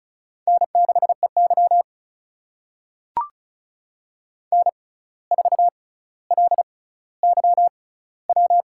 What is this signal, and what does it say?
Morse code, heard as a keyed tone, about 36 wpm, spaced as for 17 wpm, 700 Hz. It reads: N6EY N4LYW
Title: USA Call Signs - 6x Character Spacing - 35wpm